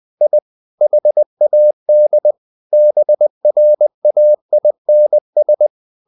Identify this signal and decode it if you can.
Morse code, heard as a keyed tone, about 20 wpm, 600 Hz. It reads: I HAD BRAINS
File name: the old man and the sea part 2 20wpm